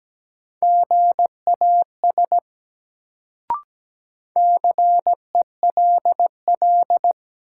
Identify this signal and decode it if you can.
Morse code, heard as a keyed tone, about 17 wpm, 700 Hz. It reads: GAS CELL
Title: Sentences from Top 1100 Words - New Word Drill - 17wpm